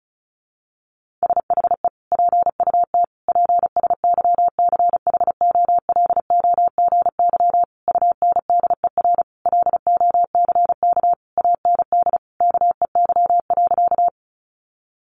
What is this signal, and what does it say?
Morse code, heard as a keyed tone, about 34 wpm, 700 Hz. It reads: SHE PUT PSYCHOLOGY UNDER LOCK AND KEY.
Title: Sentences from Top 1400 Words 35wpm